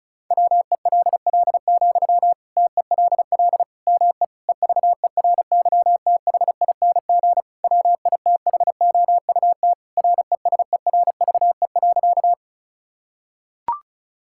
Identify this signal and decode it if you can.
Morse code, heard as a keyed tone, about 36 wpm, 700 Hz. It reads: WELL, TELL ME EVERYTHING WITHOUT RESERVE.
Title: Sentences from Top 1000 Words - Review Part 6 - 35wpm